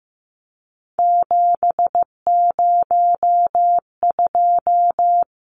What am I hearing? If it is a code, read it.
Morse code, heard as a keyed tone, about 15 wpm, 700 Hz. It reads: 702
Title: Numbers - 3-Digit 15wpm